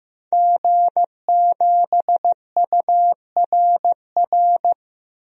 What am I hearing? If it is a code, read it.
Morse code, heard as a keyed tone, about 16 wpm, 700 Hz. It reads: G7URR
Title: DXCC-100 Call Signs - 5 Characters - Repeated 3x - 15wpm